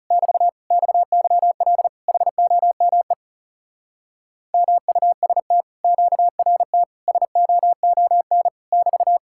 Morse code, heard as a keyed tone, about 40 wpm, 700 Hz. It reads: <BT> XYL HOME MUST QRT SOON <BT>